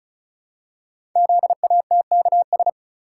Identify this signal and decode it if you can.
Morse code, heard as a keyed tone, about 36 wpm, 700 Hz. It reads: ZATKS
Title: Code Groups - 5 Letters - 35wpm